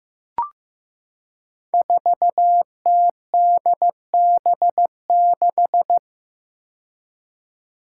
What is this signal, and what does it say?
Morse code, heard as a keyed tone, about 15 wpm, 700 Hz. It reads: 4TDB6
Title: Five Challenging Letter-Number Combinations 15wpm